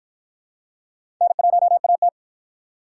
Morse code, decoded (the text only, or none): N1AT